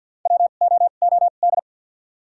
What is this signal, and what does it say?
Morse code, heard as a keyed tone, about 52 wpm, spaced as for 37 wpm, 700 Hz. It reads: WOOD
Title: 2x Character Spacing - Top 300 Words - 50wpm